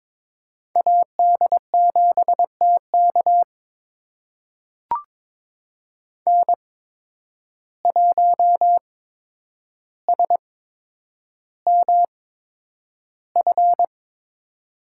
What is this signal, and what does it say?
Morse code, heard as a keyed tone, about 21 wpm, spaced as for 9 wpm, 700 Hz. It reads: AD7TK N1SMF
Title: USA Call Signs - 5 Characters - 8x Character Spacing - 22wpm